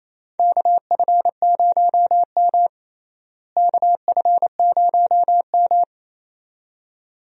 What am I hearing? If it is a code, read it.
Morse code, heard as a keyed tone, about 28 wpm, 700 Hz. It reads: KF0M KF0M
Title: USA Call Signs - 4 Characters - Repeated 2x - 28wpm